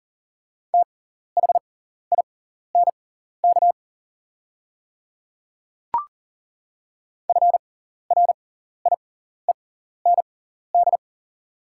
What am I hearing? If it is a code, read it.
Morse code, heard as a keyed tone, about 40 wpm, spaced as for 14 wpm, 690 Hz. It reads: THINK FRIEND
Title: 6x Character Spacing - Top 300 Words - 40wpm